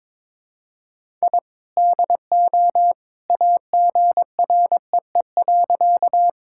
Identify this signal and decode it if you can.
Morse code, heard as a keyed tone, about 22 wpm, 700 Hz. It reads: I DO AGREE.
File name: practice / 22wpm / Sentences with 2-Word Phrases - 22wpm